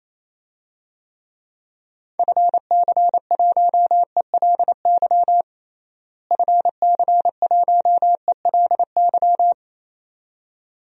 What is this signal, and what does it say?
Morse code, heard as a keyed tone, about 28 wpm, 700 Hz. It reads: FC1ELY FC1ELY
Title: DXCC-100 Call Signs - 6 Characters - Repeated 2x - 28wpm